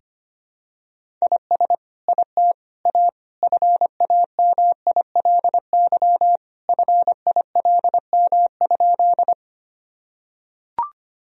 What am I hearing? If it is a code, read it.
Morse code, heard as a keyed tone, about 25 wpm, 700 Hz. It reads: IS IT A FAMILY FILM?
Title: Sentences from Top 1000 Words - Review Part 3 - 25wpm